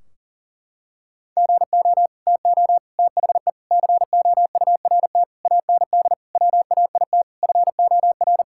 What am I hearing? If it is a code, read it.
Morse code, heard as a keyed tone, about 37 wpm, 700 Hz. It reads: GO TO THE COURT AND WAIT FOR
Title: Sentences from Top 1600 Words 40wpm